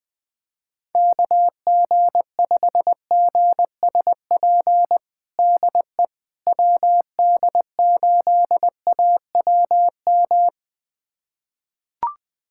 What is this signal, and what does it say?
Morse code, heard as a keyed tone, about 20 wpm, 700 Hz. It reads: KG5GSP DE WD8AWM
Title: USA Callsign DE Callsign 20wpm